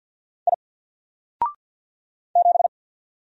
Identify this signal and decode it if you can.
Morse code, heard as a keyed tone, about 50 wpm, 700 Hz. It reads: I 7
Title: Single Character (Letter:Number:Punctuation:Prosign) 50wpm